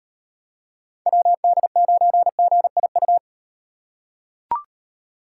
Morse code, heard as a keyed tone, about 38 wpm, 700 Hz. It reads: WD9GIU